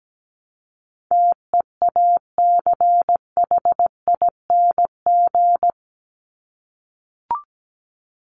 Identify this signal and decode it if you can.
Morse code, heard as a keyed tone, about 17 wpm, 700 Hz. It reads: TEACHING
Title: ING Suffix - Top 100 Words with Base Words 17wpm